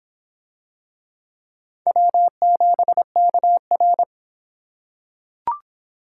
Morse code, heard as a keyed tone, about 25 wpm, 700 Hz. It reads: W7KR